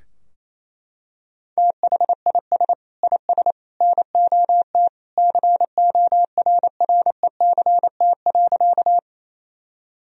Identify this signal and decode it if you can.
Morse code, heard as a keyed tone, about 28 wpm, 700 Hz. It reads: THIS IS NOT CORRECT.
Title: Sentences from Top 1000 Words - Review Part 2 - 28wpm